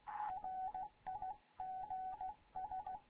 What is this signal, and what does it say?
Morse code, heard as a keyed tone, about 16 wpm, 750 Hz, with radio noise. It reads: RIGS
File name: Morze-HA2MN / hangp / 2 11 GYAKORLAT